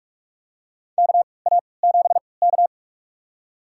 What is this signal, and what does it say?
Morse code, heard as a keyed tone, about 46 wpm, spaced as for 25 wpm, 700 Hz. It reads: KA7K
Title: USA Call Signs - 4 Characters - 3x Character Spacing - 45wpm